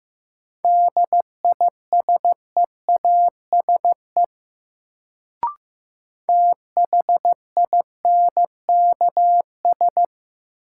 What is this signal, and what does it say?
Morse code, heard as a keyed tone, about 15 wpm, 700 Hz. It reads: DISEASE THINKS
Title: Sentences from Top 1000 Words - Review Part 6 - New Word Drill - 15wpm